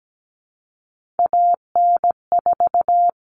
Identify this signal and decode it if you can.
Morse code, heard as a keyed tone, about 17 wpm, 700 Hz. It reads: AN4
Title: All Cut Numbers - 3-Digit 17wpm